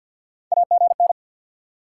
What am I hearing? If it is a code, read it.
Morse code, heard as a keyed tone, about 53 wpm, 700 Hz. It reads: AGN
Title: Top 90 QSO Elements 50wpm